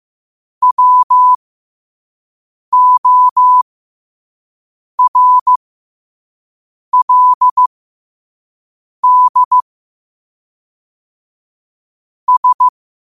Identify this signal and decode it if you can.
Morse code, heard as a keyed tone, about 15 wpm, spaced as for 5 wpm, 1000 Hz. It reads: WORLD S
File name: qotdfiles / 5wpm